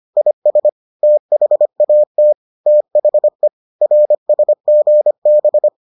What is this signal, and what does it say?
Morse code, heard as a keyed tone, about 25 wpm, 600 Hz. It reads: IS THAT THE RSGB